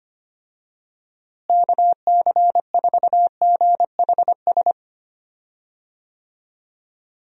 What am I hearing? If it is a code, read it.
Morse code, heard as a keyed tone, about 25 wpm, 700 Hz. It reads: KC4GHS
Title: USA Call Signs - 6 Characters - Repeated 2x - 25wpm